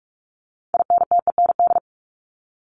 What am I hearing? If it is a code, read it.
Morse code, heard as a keyed tone, about 44 wpm, 700 Hz. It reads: INTEND
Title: Top 1501-1750 Words 45wpm